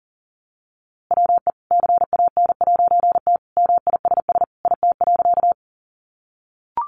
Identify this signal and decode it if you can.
Morse code, heard as a keyed tone, about 40 wpm, 700 Hz. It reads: WE CAN'T MISS IT.
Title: A Sound Of Thunder 40wpm